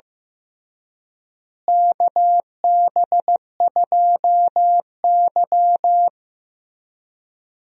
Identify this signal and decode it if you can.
Morse code, heard as a keyed tone, about 15 wpm, 700 Hz. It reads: KB2Y
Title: USA Call Signs - 4 Characters - Repeated 3x - 15wpm